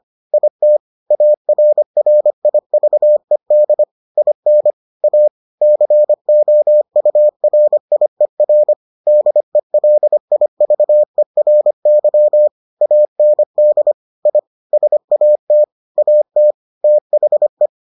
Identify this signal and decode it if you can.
Morse code, heard as a keyed tone, about 25 wpm, 600 Hz. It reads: IT ARRIVED IN A COURIER DELIVERY AND I SAT AT THE